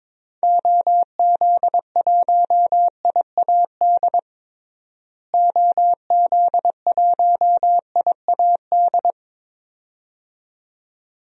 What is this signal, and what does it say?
Morse code, heard as a keyed tone, about 22 wpm, 700 Hz. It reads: OZ1IAD OZ1IAD
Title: DXCC-100 Call Signs - 6 Characters - Repeated 2x - 22wpm